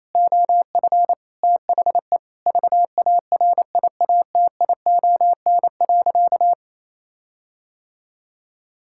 Morse code, heard as a keyed tone, about 28 wpm, 700 Hz. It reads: OF THE VARIATION.
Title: Sentences from Top 1600 Words 28wpm